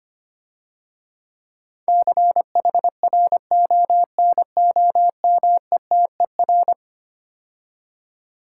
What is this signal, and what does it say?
Morse code, heard as a keyed tone, about 25 wpm, 690 Hz. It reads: CHRONOMETER
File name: practice / 25wpm / Difficult Words - Speed-Racing with Hints - 25wpm